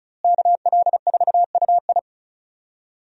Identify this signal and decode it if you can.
Morse code, heard as a keyed tone, about 36 wpm, 700 Hz. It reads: KL4UI
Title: USA Call Signs - 5 Characters - Repeated 3x - 35wpm